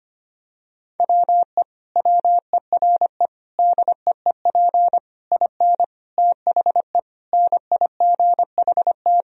Morse code, heard as a keyed tone, about 25 wpm, 700 Hz. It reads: WE WERE DEEP IN THE NIGHT
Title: Sentences from Top 1000 Words - Review Part 3 - 25wpm